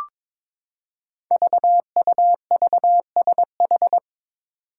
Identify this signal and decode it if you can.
Morse code, heard as a keyed tone, about 22 wpm, 700 Hz. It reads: VUVSH
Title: Five Challenging Letter-Number Combinations 22wpm